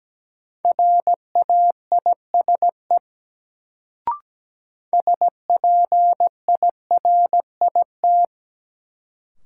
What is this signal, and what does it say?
Morse code, heard as a keyed tone, about 17 wpm, 700 Hz. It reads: RAISE SPIRIT
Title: Sentences from Top 1400 Words - New Word Drill - 17wpm